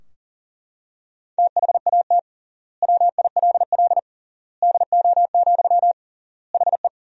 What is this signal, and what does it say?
Morse code, heard as a keyed tone, about 40 wpm, 700 Hz. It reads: THAT WILL DO, HE
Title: 3x Word-Spacing - Sentences from Top 100 Words - 40wpm